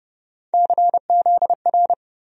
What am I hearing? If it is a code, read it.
Morse code, heard as a keyed tone, about 30 wpm, 700 Hz. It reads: CZR
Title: USA Call Sign Postfixes - 3 Letters 30wpm